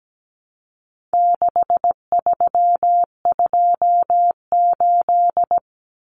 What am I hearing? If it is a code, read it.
Morse code, heard as a keyed tone, about 17 wpm, 700 Hz. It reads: 6328
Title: Common Cut Numbers - 4-Digit 17wpm